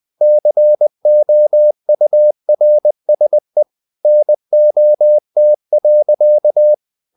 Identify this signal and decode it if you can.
Morse code, heard as a keyed tone, about 20 wpm, 600 Hz. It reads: COURSE NOT.
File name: the old man and the sea part 2 20wpm